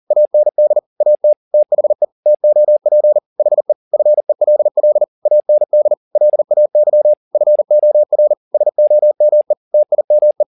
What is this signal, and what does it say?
Morse code, heard as a keyed tone, about 40 wpm, 600 Hz. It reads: AND AT THE TOP HE FELL AND LAY FOR SOME TIME
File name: the old man and the sea 2 40wpm